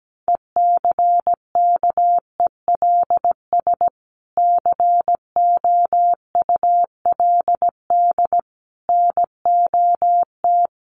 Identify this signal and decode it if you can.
Morse code, heard as a keyed tone, about 17 wpm, 700 Hz. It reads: ECKELS COULD NOT
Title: A Sound Of Thunder 17wpm